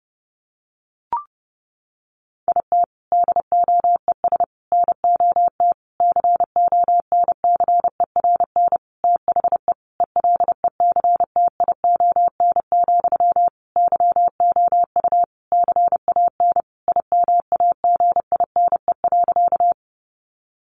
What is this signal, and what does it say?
Morse code, heard as a keyed tone, about 30 wpm, 700 Hz. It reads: IT DOES NOT CONCERN THE ELECTION, YOU CAN IMAGINE.
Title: Sentences from Top 1500 Words 30wpm